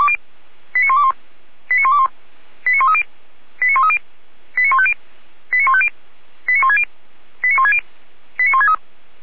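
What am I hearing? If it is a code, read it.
Five-tone selective call (ZVEI), heard as quick runs of five tones, tones of about 80 ms. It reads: 89121 89121 89130 89130 89150 89150 89160 89160 89163
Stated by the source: ZVEI five-tone selective call for address 89140, received by SDR, from the feuerwehr-orlamuende (Orlamünde fire brigade) audio server.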